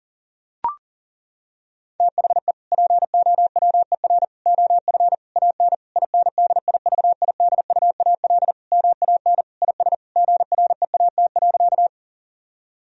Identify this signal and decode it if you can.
Morse code, heard as a keyed tone, about 40 wpm, 700 Hz. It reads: THE POWER OF AN INDIVIDUAL MAN IS GREAT.
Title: Sentences from Top 1000 Words - Review Part 4 - 40wpm